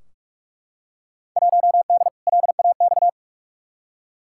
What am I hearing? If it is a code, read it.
Morse code, heard as a keyed tone, about 43 wpm, 700 Hz. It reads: 1D LAX
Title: Field Day Exchanges - 45wpm